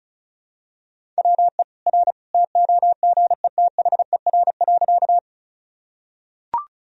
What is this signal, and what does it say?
Morse code, heard as a keyed tone, about 35 wpm, 700 Hz. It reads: WE R TOGETHER.
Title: Sentences from Top 300 Words and Abbreviations 35wpm